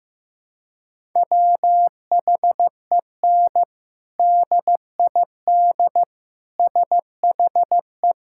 Morse code, heard as a keyed tone, about 16 wpm, 700 Hz. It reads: WHEN DID SHE
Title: Sentences from Top 1000 Words - Review Part 6 - 15wpm